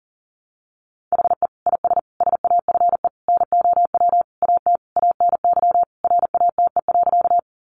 Morse code, heard as a keyed tone, about 40 wpm, 700 Hz. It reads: HE IS SAFE NOW AT ANY RATE.
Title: Sentences from Top 900 Words 40wpm